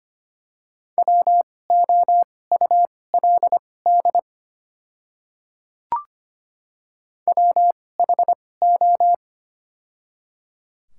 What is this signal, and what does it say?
Morse code, heard as a keyed tone, about 25 wpm, spaced as for 18 wpm, 700 Hz. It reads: WOULD WHO
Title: 2x Character Spacing - Top 200 Words - 25wpm